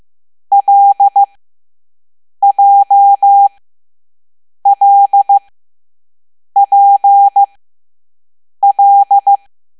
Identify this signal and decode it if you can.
Morse code, heard as a keyed tone, about 15 wpm, spaced as for 6 wpm, 800 Hz. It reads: LJLPL